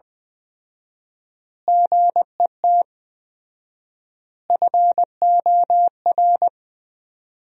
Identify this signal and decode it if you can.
Morse code, heard as a keyed tone, about 20 wpm, 700 Hz. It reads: GET FOR
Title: 4x Word-Spacing - Sentences from Top 400 Words - 20wpm